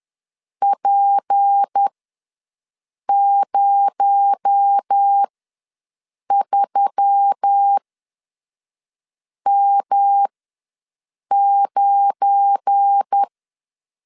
Morse code, heard as a keyed tone, about 10 wpm, spaced as for 5 wpm, 800 Hz. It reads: P03M9
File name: LEZIONI / 8 YJ45 / 8 YJ4 ws 10wpm 4020